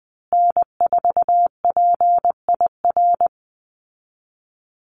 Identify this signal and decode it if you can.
Morse code, heard as a keyed tone, about 19 wpm, 700 Hz. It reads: N4PIR
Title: Call Signs Repeated 2x - 20wpm